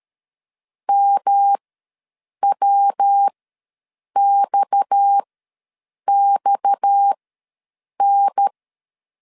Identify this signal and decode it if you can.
Morse code, heard as a keyed tone, about 13 wpm, spaced as for 7 wpm, 800 Hz. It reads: MWXXN
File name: LEZIONI / 6 PKX0 / 6 PKX ws 12wpm 3106